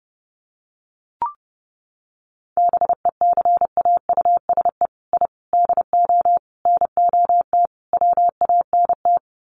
Morse code, heard as a keyed tone, about 30 wpm, 700 Hz. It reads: BECAUSE I DO NOT WANT
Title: Sets of 5 Words Encore 30wpm